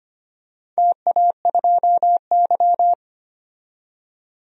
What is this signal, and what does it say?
Morse code, heard as a keyed tone, about 25 wpm, 700 Hz. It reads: TA2Y
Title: DXCC-100 Call Signs - 4 Characters - 25wpm